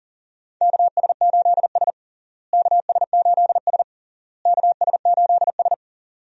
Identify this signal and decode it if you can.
Morse code, heard as a keyed tone, about 37 wpm, 700 Hz. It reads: KS8S KS8S KS8S
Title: USA Call Signs - 4 Characters - Repeated 3x - 40wpm